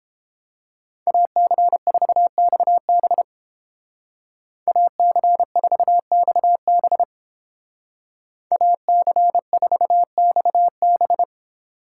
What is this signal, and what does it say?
Morse code, heard as a keyed tone, about 29 wpm, 700 Hz. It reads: AC4XB AC4XB AC4XB